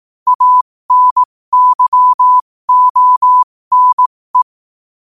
Morse code, heard as a keyed tone, about 18 wpm, spaced as for 15 wpm, 1000 Hz. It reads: ANYONE
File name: qotdfiles / 15wpm